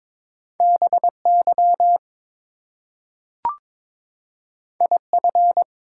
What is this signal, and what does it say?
Morse code, heard as a keyed tone, about 22 wpm, 700 Hz. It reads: BY IF
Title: Two Letter Words - 22wpm